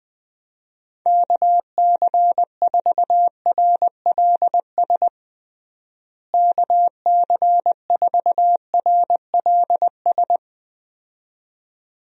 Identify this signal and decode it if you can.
Morse code, heard as a keyed tone, about 20 wpm, 700 Hz. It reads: KC4RLS KC4RLS
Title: USA Call Signs - 6 Characters - Repeated 2x - 20wpm